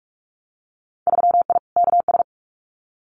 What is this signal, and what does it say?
Morse code, heard as a keyed tone, about 44 wpm, 700 Hz. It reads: 3I KS
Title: Winter Field Day Exchanges 45wpm